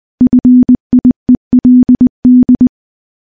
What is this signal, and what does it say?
Morse code, heard as a keyed tone, about 20 wpm, 250 Hz. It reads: FIELD
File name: morse / morse-training-20WPM